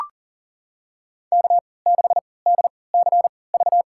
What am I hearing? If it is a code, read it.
Morse code, heard as a keyed tone, about 40 wpm, spaced as for 23 wpm, 700 Hz. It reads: K6DCV
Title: USA Call Signs - 3x Character Spacing - 40wpm